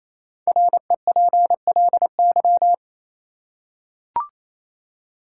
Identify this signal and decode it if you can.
Morse code, heard as a keyed tone, about 28 wpm, 700 Hz. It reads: REPLY
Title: Five Letter Words - 28wpm